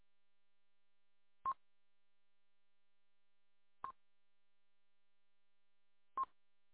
There is no code.